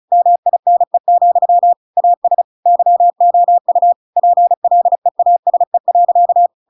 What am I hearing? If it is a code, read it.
Morse code, heard as a keyed tone, about 36 wpm, 700 Hz. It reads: MINE, AS YOU PLEASE.